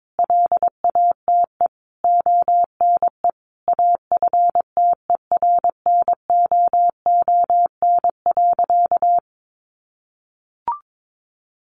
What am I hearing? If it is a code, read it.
Morse code, heard as a keyed tone, about 22 wpm, 700 Hz. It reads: LATE ONE AFTERNOON.
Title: Sentences from Top 1500 Words 22wpm